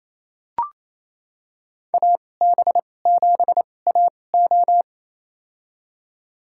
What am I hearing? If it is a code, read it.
Morse code, heard as a keyed tone, about 28 wpm, spaced as for 20 wpm, 700 Hz. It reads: AB7AO
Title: USA Call Signs - 5 Characters - 2x Character Spacing - 28wpm